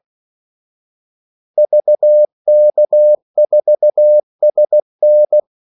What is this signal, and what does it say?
Morse code, heard as a keyed tone, about 16 wpm, 600 Hz. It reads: VK4SN
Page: Kaldesignaler 16 wpm